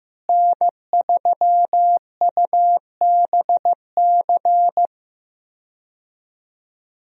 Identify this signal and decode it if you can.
Morse code, heard as a keyed tone, about 15 wpm, 700 Hz. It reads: N3UBC
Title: USA Call Signs - 5 Characters - Repeated 3x - 15wpm